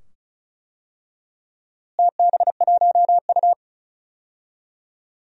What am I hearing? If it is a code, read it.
Morse code, heard as a keyed tone, about 36 wpm, 700 Hz. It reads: TB1U
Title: DXCC-100 Call Signs - 4 Characters - Repeated 2x - 35wpm